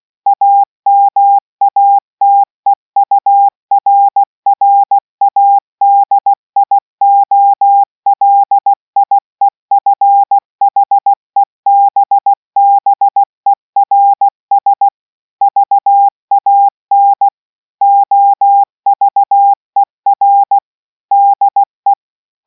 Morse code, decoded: AMATEURRADIOLIEFHEBBERS VAN OVER DE